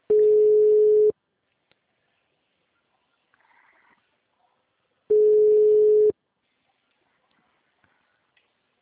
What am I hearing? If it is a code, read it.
Ringback tone (Europe, 425 Hz).
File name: out- -2009- - - .17